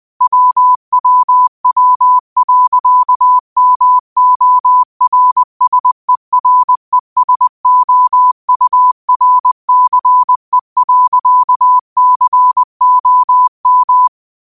Morse code, heard as a keyed tone, about 20 wpm, 1000 Hz. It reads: WWW.MORSERESOURCE.COM